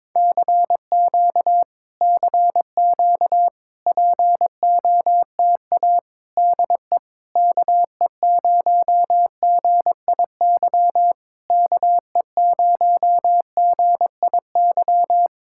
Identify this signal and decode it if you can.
Morse code, heard as a keyed tone, about 22 wpm, 700 Hz. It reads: CQ CQ POTA DE KE0GIY KE0GIY
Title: Speed-Racing Common QSO Phrases - 22wpm